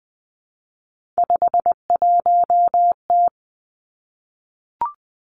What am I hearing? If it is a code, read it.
Morse code, heard as a keyed tone, about 20 wpm, 700 Hz. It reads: 51T
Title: Three Challenging Letter-Number Combinations 20wpm